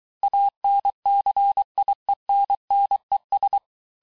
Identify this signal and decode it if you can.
Morse code, heard as a keyed tone, about 23 wpm, 800 Hz. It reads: ANCIENNES
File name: cours-du-mercredi / audios / COURS-855--texte-clair-vitesse-22-mpm